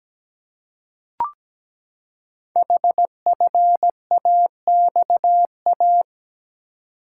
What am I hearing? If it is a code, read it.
Morse code, heard as a keyed tone, about 17 wpm, 700 Hz. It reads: HFAXA
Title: Code Groups - 5 Letters - 17wpm